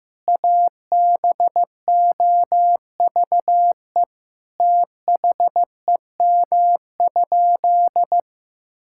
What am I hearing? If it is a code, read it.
Morse code, heard as a keyed tone, about 15 wpm, 700 Hz. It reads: ABOVE THEM?
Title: Sentences from Top 1000 Words - Review Part 5 - 15wpm